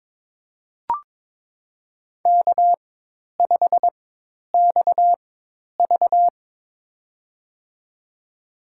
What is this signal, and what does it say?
Morse code, heard as a keyed tone, about 22 wpm, spaced as for 10 wpm, 700 Hz. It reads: K5XV
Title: USA Call Signs - 4 Characters - 4x Character Spacing - 22wpm